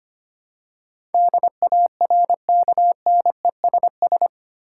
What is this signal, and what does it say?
Morse code, heard as a keyed tone, about 25 wpm, 700 Hz. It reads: DARKNESS
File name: Eight Letter Words - 25wpm